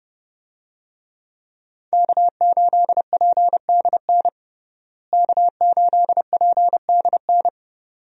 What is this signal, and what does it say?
Morse code, heard as a keyed tone, about 30 wpm, 700 Hz. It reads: K8PDN K8PDN
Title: USA Call Signs - 5 Characters - Repeated 2x - 30wpm